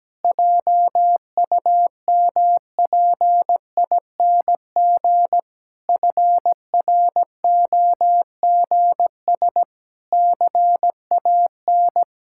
Morse code, heard as a keyed tone, about 17 wpm, 700 Hz. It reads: JUMPING FROGS CAN
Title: Pangram Challenge v2 - 17wpm